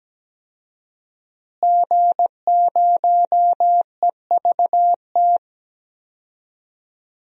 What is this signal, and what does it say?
Morse code, heard as a keyed tone, about 17 wpm, 700 Hz. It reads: G0EVT
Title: DXCC-100 Call Signs - 5 Characters - Repeated 3x - 17wpm